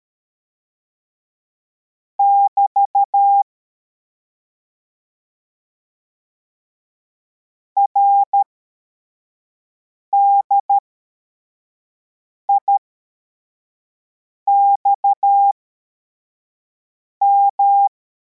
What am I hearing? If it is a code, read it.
Morse code, heard as a keyed tone, about 13 wpm, spaced as for 4 wpm, 800 Hz. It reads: <BT> RDIXM
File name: formation / cw / Text34